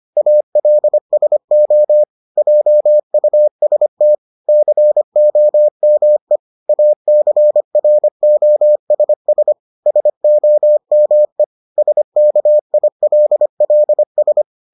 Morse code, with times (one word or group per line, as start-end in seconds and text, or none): 0.2-2.0 ALSO
2.4-4.2 JUST
4.5-6.4 COME
6.7-9.5 ACROSS
9.9-11.4 SOME
11.8-14.4 SKILLS